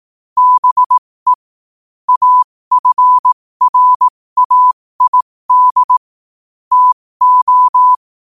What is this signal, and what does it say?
Morse code, heard as a keyed tone, about 18 wpm, spaced as for 15 wpm, 1000 Hz. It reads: BE AFRAID TO
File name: 15wpm